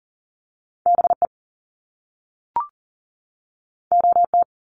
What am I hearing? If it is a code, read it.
Morse code, heard as a keyed tone, about 40 wpm, 700 Hz. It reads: BE OT
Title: Top Two Letter Combinations in Words 40wpm